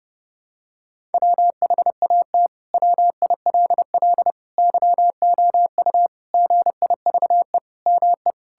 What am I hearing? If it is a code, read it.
Morse code, heard as a keyed tone, about 30 wpm, 700 Hz. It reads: WHAT WILL YOU GIVE ME
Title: Sentences from Top 1000 Words - Review Part 6 - 30wpm